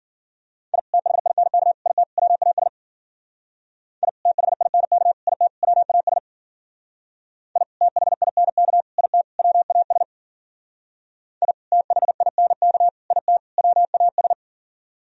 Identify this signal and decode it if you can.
Morse code, heard as a keyed tone, about 52 wpm, 700 Hz. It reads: I THINK IT WAS I THINK IT WAS I THINK IT WAS I THINK IT WAS